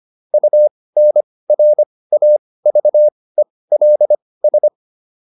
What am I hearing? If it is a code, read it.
Morse code, heard as a keyed tone, about 25 wpm, spaced as for 18 wpm, 600 Hz. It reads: UNRAVELS